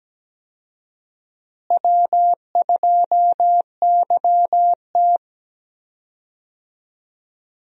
Morse code, heard as a keyed tone, about 17 wpm, 700 Hz. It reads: W2YT
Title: USA Call Signs - 4 Characters - Repeated 3x - 17wpm